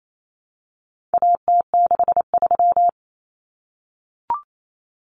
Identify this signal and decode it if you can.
Morse code, heard as a keyed tone, about 28 wpm, 700 Hz. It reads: AT63
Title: Common Cut Numbers - 4-Digit 28wpm